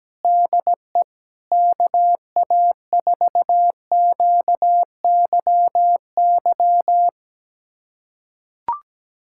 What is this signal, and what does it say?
Morse code, heard as a keyed tone, about 17 wpm, 700 Hz. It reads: DE KA4QYY
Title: USA Callsign DE Callsign 17wpm